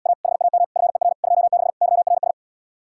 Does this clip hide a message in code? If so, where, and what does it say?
Yes: Morse code from 0.1 to 2.3 s, heard as a keyed tone, about 80 wpm, 700 Hz. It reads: A HAM FER 26 YRS